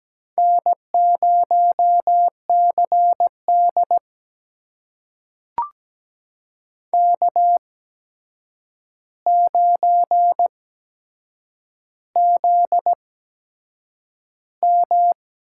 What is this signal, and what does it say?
Morse code, heard as a keyed tone, about 17 wpm, spaced as for 7 wpm, 700 Hz. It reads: N0CD K9ZM
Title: USA Call Signs - 4 Characters - 8x Character Spacing - 17wpm